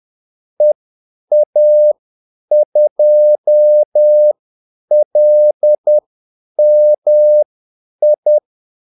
Morse code, heard as a keyed tone, about 10 wpm, spaced as for 8 wpm, 600 Hz. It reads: EA2LMI
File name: Callsigns-10-8wpm